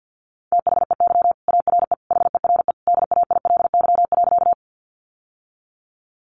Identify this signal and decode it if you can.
Morse code, heard as a keyed tone, about 50 wpm, 700 Hz. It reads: THEY ARE HERE DAILY.
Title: Sentences from Top 1100 Words 50wpm